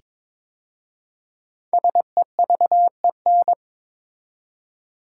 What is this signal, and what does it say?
Morse code, heard as a keyed tone, about 22 wpm, 690 Hz. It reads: SEVEN
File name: 5 Letter Words - First 3 Letters Given - 22wpm